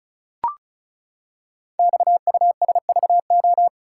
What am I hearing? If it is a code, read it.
Morse code, heard as a keyed tone, about 36 wpm, 700 Hz. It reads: XUSVO
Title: Code Groups - 5 Letter:Numbers - 35wpm